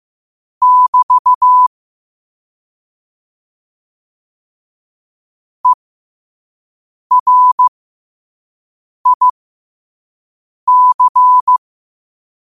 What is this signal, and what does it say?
Morse code, heard as a keyed tone, about 15 wpm, spaced as for 5 wpm, 1000 Hz. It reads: <BT> ERIC
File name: qotdfiles / 5wpm